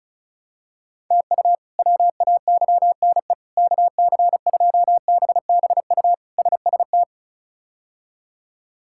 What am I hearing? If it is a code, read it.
Morse code, heard as a keyed tone, about 34 wpm, 700 Hz. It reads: TU WAYNE KC2BBU SST
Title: K1USN SST QSO Practice - 35wpm